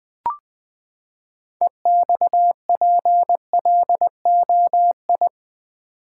Morse code, heard as a keyed tone, about 20 wpm, 700 Hz. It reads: EXPLOI
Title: Build A Word - 6 Letters At A Time - 12-Letter Words - 20wpm